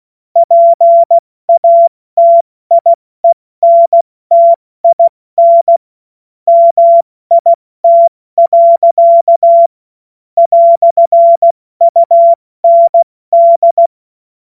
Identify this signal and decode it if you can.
Morse code, heard as a keyed tone, about 16 wpm, 680 Hz. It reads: PATIENTIN MIT. "UND
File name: Morsekurs / Witze / Witz 27. Juni 2023 70 F80